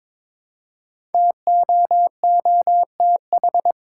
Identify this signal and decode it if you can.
Morse code, heard as a keyed tone, about 23 wpm, 700 Hz. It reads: TOOTH
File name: 5 Letter Words - First 3 Letters Given - 22wpm